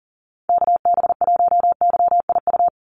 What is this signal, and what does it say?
Morse code, heard as a keyed tone, about 40 wpm, 700 Hz. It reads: KB1YIU
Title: USA Call Signs 40wpm